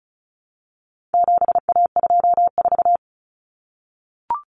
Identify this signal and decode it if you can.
Morse code, heard as a keyed tone, about 34 wpm, 700 Hz. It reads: 7A24
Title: Common Cut Numbers - 4-Digit 35wpm